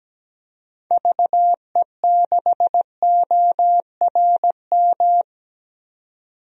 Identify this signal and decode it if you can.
Morse code, heard as a keyed tone, about 17 wpm, 700 Hz. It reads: VE6ORM
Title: DXCC-100 Call Signs - 6 Characters - 17wpm